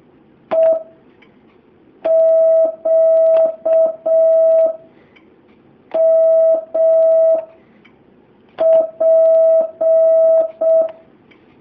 Morse code, heard as a keyed tone, about 6 wpm, 650 Hz. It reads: E Q M P
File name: jh8cbh / 2019cw